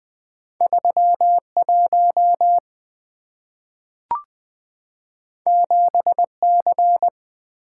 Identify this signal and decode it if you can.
Morse code, heard as a keyed tone, about 20 wpm, 690 Hz. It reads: 31 7C
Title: Two Letter-Number Combinations - 20wpm